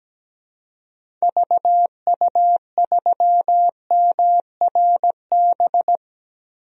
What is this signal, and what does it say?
Morse code, heard as a keyed tone, about 17 wpm, 700 Hz. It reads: VU3MRB
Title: DXCC-100 Call Signs - 6 Characters - 17wpm